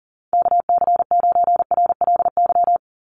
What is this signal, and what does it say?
Morse code, heard as a keyed tone, about 40 wpm, 700 Hz. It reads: KC9RLY
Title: USA Call Signs 40wpm